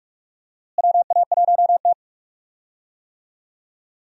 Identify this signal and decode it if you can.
Morse code, heard as a keyed tone, about 46 wpm, 700 Hz. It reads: WA1T
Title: USA Call Signs - 4 Characters - Repeated 3x - 45wpm